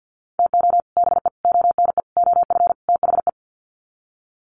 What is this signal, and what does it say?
Morse code, heard as a keyed tone, about 50 wpm, 700 Hz. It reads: TO BE ONE OF THE
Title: Sets of 5 Words 50wpm